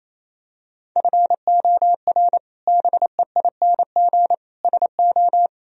Morse code, heard as a keyed tone, about 27 wpm, 700 Hz. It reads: FOR BEING SO
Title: Sentences from Top 1000 Words - Review Part 3 - 28wpm